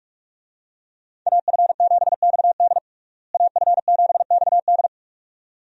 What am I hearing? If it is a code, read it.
Morse code, heard as a keyed tone, about 45 wpm, 700 Hz. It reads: AF7XD AF7XD
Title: USA Call Signs - 5 Characters - Repeated 2x - 45wpm